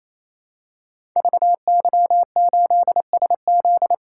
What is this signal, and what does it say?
Morse code, heard as a keyed tone, about 29 wpm, 700 Hz. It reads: VY8SZ
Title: Code Groups - 5 Letter:Numbers - 28wpm